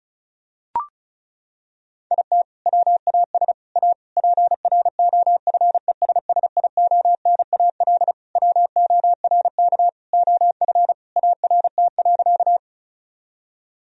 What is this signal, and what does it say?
Morse code, heard as a keyed tone, about 36 wpm, 700 Hz. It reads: IT WAS A PROFESSIONAL WORK OF ART.
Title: Sentences from Top 1000 Words - Review Part 4 - 35wpm